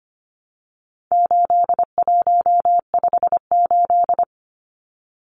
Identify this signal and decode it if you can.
Morse code, heard as a keyed tone, about 25 wpm, 700 Hz. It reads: 8158
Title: Numbers - 4-Digit 25wpm